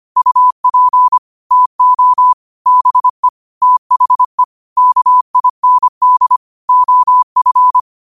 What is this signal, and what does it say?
Morse code, heard as a keyed tone, about 25 wpm, 1000 Hz. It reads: UP TO BE THE KIND OF
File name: 25wpm